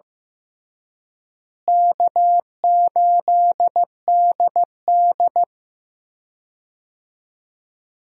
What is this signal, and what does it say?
Morse code, heard as a keyed tone, about 15 wpm, 700 Hz. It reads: K8DD
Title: USA Call Signs - 4 Characters - Repeated 3x - 15wpm